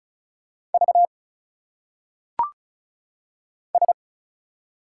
Morse code, heard as a keyed tone, about 36 wpm, 700 Hz. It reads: V S
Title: Single Letters - Phonetic Pronunciation 35wpm